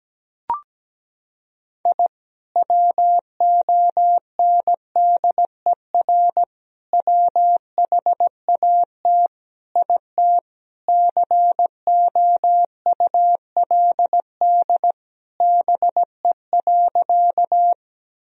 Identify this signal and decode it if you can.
Morse code, heard as a keyed tone, about 17 wpm, 700 Hz. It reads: I WONDER WHAT IT COULD BE.
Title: Sentences from Top 1000 Words - Review Part 3 - 17wpm